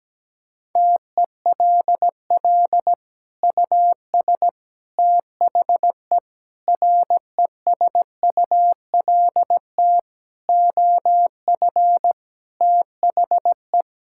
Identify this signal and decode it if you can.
Morse code, heard as a keyed tone, about 17 wpm, 700 Hz. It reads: TELL US THE RESULT OF THE
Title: Sentences from Top 1600 Words 17wpm